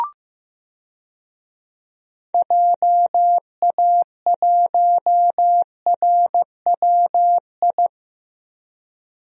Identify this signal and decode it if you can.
Morse code, heard as a keyed tone, about 15 wpm, 700 Hz. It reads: JA1RWI
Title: USA Call Signs - 6 Characters - Repeated 2x - 15wpm